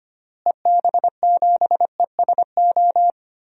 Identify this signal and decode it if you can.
Morse code, heard as a keyed tone, about 25 wpm, 700 Hz. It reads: EB7ESO
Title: DXCC-100 Call Signs - 6 Characters - Repeated 2x - 25wpm